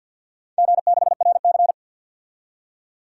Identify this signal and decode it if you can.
Morse code, heard as a keyed tone, about 51 wpm, 700 Hz. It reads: G6RC
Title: DXCC-100 Call Signs - 4 Characters - 50wpm